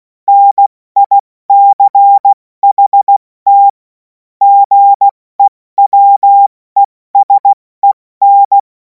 Morse code, heard as a keyed tone, about 16 wpm, 800 Hz. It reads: NICHT GEWESEN
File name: Morsekurs / Witze / Witz 10. Februar 2026 70 F80